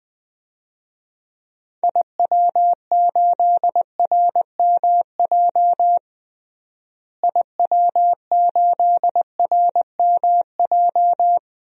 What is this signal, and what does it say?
Morse code, heard as a keyed tone, about 20 wpm, 700 Hz. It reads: IW8RMJ IW8RMJ
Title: DXCC-100 Call Signs - 6 Characters - Repeated 2x - 20wpm